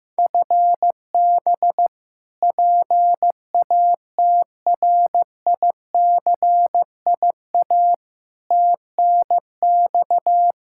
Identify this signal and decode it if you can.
Morse code, heard as a keyed tone, about 15 wpm, 700 Hz. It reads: FB PATRICIA TNX
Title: Practice QSO - Exchange 3 - More Info - 15wpm